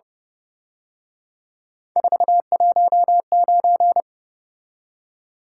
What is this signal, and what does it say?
Morse code, heard as a keyed tone, about 29 wpm, 700 Hz. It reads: 419
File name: Speed-Racing - Three Letter-Number Combinations - 25wpm